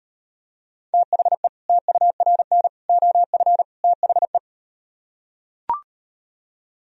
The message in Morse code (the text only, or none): THE TURN OF THE